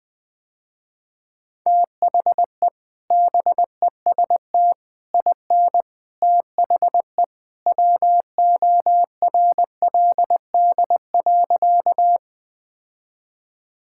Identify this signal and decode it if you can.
Morse code, heard as a keyed tone, about 20 wpm, 700 Hz. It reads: THE BEST IN THE WORLD.
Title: Sentences with 3-Word Phrases - Spoken Beginning - 20wpm